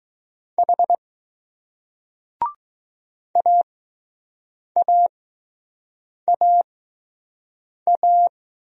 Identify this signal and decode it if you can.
Morse code, heard as a keyed tone, about 20 wpm, spaced as for 6 wpm, 700 Hz. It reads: H AAAA